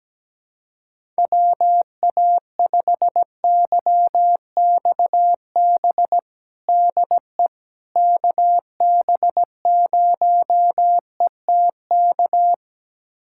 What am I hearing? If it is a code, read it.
Morse code, heard as a keyed tone, about 17 wpm, 700 Hz. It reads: WA5YXB DE KB0ETK
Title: USA Callsign DE Callsign - 6 Characters - 17wpm